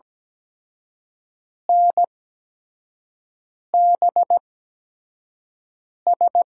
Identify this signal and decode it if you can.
Morse code, heard as a keyed tone, about 17 wpm, spaced as for 5 wpm, 700 Hz. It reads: NBS
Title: USA Call Signs - 5 Characters - 8x Character Spacing - 17wpm